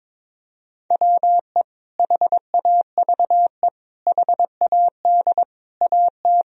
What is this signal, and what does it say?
Morse code, heard as a keyed tone, about 22 wpm, 700 Hz. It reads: WE HAVE HAD AT
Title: Sentences from Top 1000 Words - Review Part 1 - 22wpm